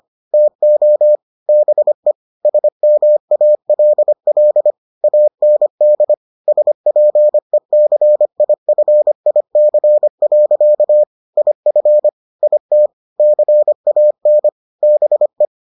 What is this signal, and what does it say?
Morse code, heard as a keyed tone, about 25 wpm, 600 Hz. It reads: TO BE SMALL AND SPECIFIC. IF IT CAN BE